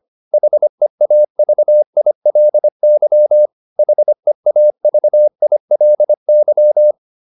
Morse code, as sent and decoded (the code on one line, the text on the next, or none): .... .- ...- .. .-.. -.-- / .... .- ...- .. .-.. -.--
HEAVILY HEAVILY